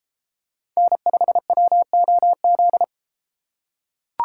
Morse code, heard as a keyed tone, about 31 wpm, 700 Hz. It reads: N5WOZ